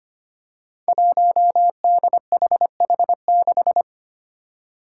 Morse code, heard as a keyed tone, about 25 wpm, 700 Hz. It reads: 1DHH6
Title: Five Challenging Letter-Number Combinations 25wpm